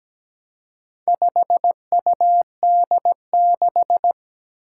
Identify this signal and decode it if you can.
Morse code, heard as a keyed tone, about 17 wpm, 700 Hz. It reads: 5UD6
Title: Four Challenging Letter-Number Combinations 17wpm